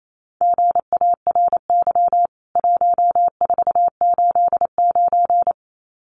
Morse code, heard as a keyed tone, about 28 wpm, 700 Hz. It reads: GARY 1489
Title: CWT Exchange 28wpm